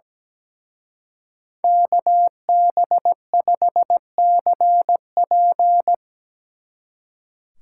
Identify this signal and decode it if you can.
Morse code, heard as a keyed tone, about 19 wpm, 700 Hz. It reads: KB5CP
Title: Speed Racing - USA 2x2 Call Signs 17wpm